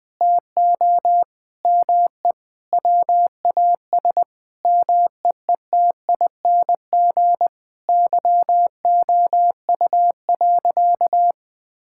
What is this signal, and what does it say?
Morse code, heard as a keyed tone, about 20 wpm, 700 Hz. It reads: TO ME WAS MEETING YOU.
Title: Sentences with 7-Word Phrases - 20wpm